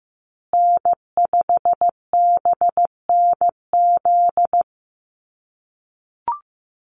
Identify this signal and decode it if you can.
Morse code, heard as a keyed tone, about 15 wpm, 700 Hz. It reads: N5BNZ
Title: USA Call Signs Repeated 3x 15wpm